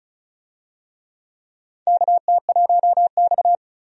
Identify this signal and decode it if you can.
Morse code, heard as a keyed tone, about 34 wpm, 700 Hz. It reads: KT1X
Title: USA Call Signs - 4 Characters - 35wpm